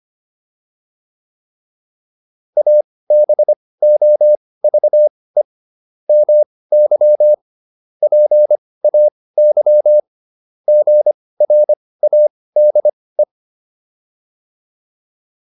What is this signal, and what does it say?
Morse code, heard as a keyed tone, about 25 wpm, spaced as for 18 wpm, 600 Hz. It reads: ABOVE MY PAY GRADE
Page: intermediate-practice-files